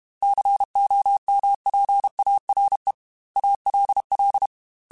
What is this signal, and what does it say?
Morse code, heard as a keyed tone, about 32 wpm, 800 Hz. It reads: COMPARE ALL
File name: Technical Subjects / Technical Subjects-mp3s / OR Gate @30